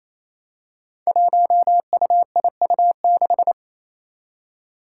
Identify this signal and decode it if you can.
Morse code, heard as a keyed tone, about 27 wpm, 700 Hz. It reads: 1UIU6
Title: Five Challenging Letter-Number Combinations 28wpm